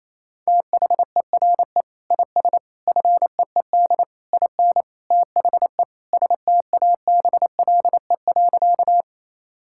Morse code, heard as a keyed tone, about 28 wpm, 700 Hz. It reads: THERE IS FEED IN THE STABLE.
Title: Sentences from Top 1600 Words 28wpm